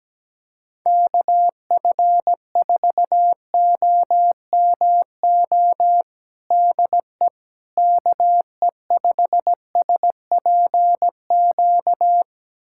Morse code, heard as a keyed tone, about 17 wpm, 700 Hz. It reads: KF4OMO DE KE5SPQ
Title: USA Callsign DE Callsign - 6 Characters - 17wpm